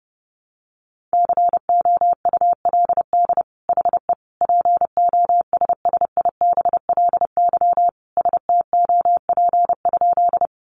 Morse code, heard as a keyed tone, about 30 wpm, 700 Hz. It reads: COULD HE POSSIBLY STOP?
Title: Sentences from Top 1300 Words 30wpm